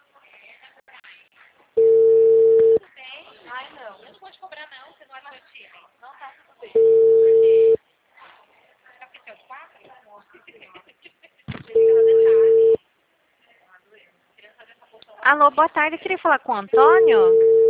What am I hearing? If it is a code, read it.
Ringback tone (China, 450 Hz), with speech.